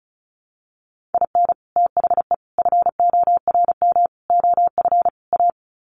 Morse code, heard as a keyed tone, about 34 wpm, 700 Hz. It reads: IN THE FORM OF A
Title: Sets of 5 Words 35wpm